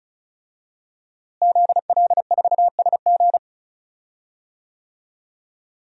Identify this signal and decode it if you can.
Morse code, heard as a keyed tone, about 36 wpm, 700 Hz. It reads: ZL4SG
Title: DXCC-100 Call Signs - 5 Characters - Repeated 3x - 35wpm